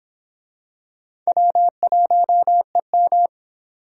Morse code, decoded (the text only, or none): W1EM